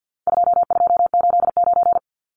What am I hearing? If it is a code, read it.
Morse code, heard as a keyed tone, about 50 wpm, 700 Hz. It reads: 2289
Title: Numbers - 4-Digit 50wpm